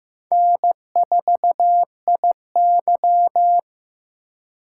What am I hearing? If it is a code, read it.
Morse code, heard as a keyed tone, about 18 wpm, 700 Hz. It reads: N4IY
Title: Speed Racing - USA 1x2 Call Signs 15wpm